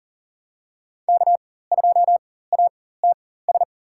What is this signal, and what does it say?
Morse code, heard as a keyed tone, about 40 wpm, spaced as for 19 wpm, 700 Hz. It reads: K2ATS
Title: USA Call Signs - 5 Characters - 4x Character Spacing - 40wpm